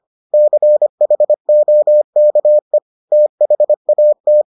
Morse code, heard as a keyed tone, about 25 wpm, 600 Hz. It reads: CHOKE THAT